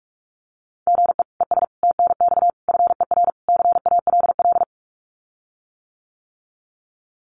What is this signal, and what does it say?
Morse code, heard as a keyed tone, about 46 wpm, 700 Hz. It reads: GE ES TNX FER CALL
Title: Common QSO Phrases Encore 45wpm